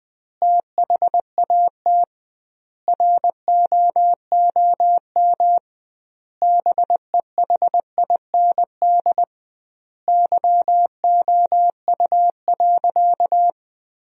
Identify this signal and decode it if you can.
Morse code, heard as a keyed tone, about 20 wpm, 700 Hz. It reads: THAT ROOM BEHIND YOU.
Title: 2x Word-Spacing - Sentences from Top 500 Words - 20wpm